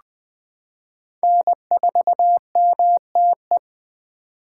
Morse code, heard as a keyed tone, about 20 wpm, 700 Hz. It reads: N4MTE
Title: 2023 POTA Activator Call Signs - 20wpm